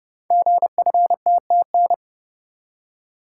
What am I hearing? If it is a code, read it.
Morse code, heard as a keyed tone, about 28 wpm, 700 Hz. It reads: GFTTN
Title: Code Groups - 5 Letters - 30wpm